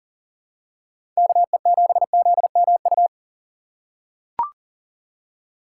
Morse code, heard as a keyed tone, about 40 wpm, 700 Hz. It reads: KE7ZMU
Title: USA Call Signs - 3x Character Spacing - 40wpm